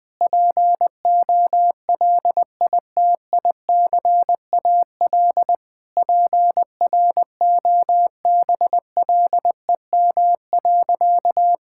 Morse code, heard as a keyed tone, about 21 wpm, 700 Hz. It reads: POLITICAL PROBLEM.
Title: Sentences from Top 1000 Words - Review Part 5 - 20wpm